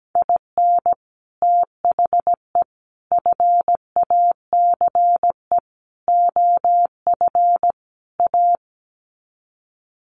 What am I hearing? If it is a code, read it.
Morse code, heard as a keyed tone, about 17 wpm, 700 Hz. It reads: IN THE FACE OF A
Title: Sets of 5 Words Encore 17wpm